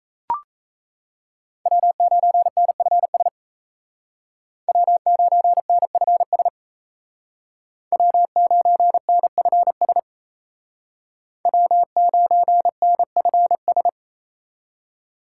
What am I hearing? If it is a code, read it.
Morse code, heard as a keyed tone, about 33 wpm, 700 Hz. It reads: W9NFS W9NFS W9NFS W9NFS
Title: Speed Racing - USA 1x3 Call Signs 28wpm